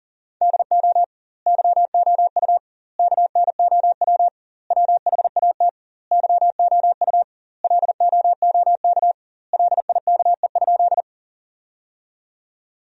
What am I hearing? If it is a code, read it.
Morse code, heard as a keyed tone, about 40 wpm, 700 Hz. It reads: DO YOU KNOW WHAT YOU LOOK LIKE?
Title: 2x Word-Spacing - Sentences from Top 100 Words - 40wpm